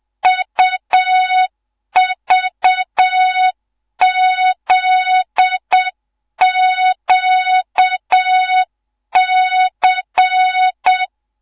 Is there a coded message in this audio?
Yes: Morse code from 0.3 to 11.0 s, heard as a keyed tone, about 7 wpm, 3700 Hz. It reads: UVZQC